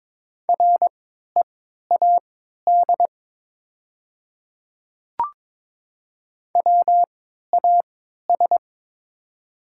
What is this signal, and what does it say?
Morse code, heard as a keyed tone, about 22 wpm, spaced as for 13 wpm, 700 Hz. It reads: READ WAS
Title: 3x Character Spacing - Top 500 Words - 22wpm